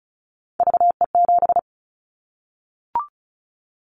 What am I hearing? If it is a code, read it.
Morse code, heard as a keyed tone, about 34 wpm, 700 Hz. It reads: VE7
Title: Callsign Prefixes for Top 100 DXCC - 35wpm